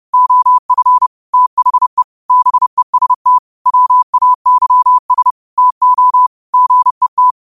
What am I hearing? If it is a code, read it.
Morse code, heard as a keyed tone, about 31 wpm, 1000 Hz. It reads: OF THE BEST WAYS TO GET